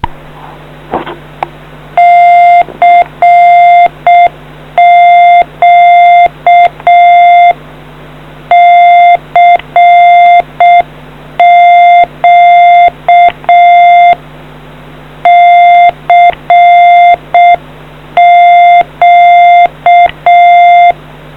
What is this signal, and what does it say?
Morse code, heard as a keyed tone, about 6 wpm, 700 Hz. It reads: CQ CQ CQ